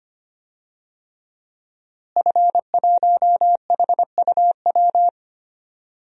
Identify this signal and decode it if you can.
Morse code, heard as a keyed tone, about 25 wpm, 700 Hz. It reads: F1HUW
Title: DXCC-100 Call Signs - 5 Characters - Repeated 3x - 25wpm